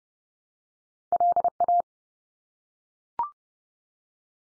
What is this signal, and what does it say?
Morse code, heard as a keyed tone, about 30 wpm, 700 Hz. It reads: LA
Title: US States 30WPM